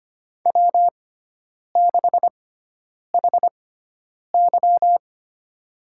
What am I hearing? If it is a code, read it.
Morse code, heard as a keyed tone, about 25 wpm, spaced as for 9 wpm, 700 Hz. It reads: W6HY
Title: USA Call Signs - 4 Characters - 6x Character Spacing - 25wpm